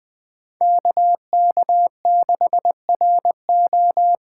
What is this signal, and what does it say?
Morse code, heard as a keyed tone, about 20 wpm, 700 Hz. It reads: KK6RO
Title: USA Call Signs - 5 Characters - 4x Character Spacing - 20wpm